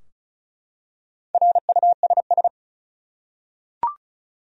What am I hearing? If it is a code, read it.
Morse code, heard as a keyed tone, about 35 wpm, 700 Hz. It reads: RUSS
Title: USA Common Names - Max 4 Letters - 35wpm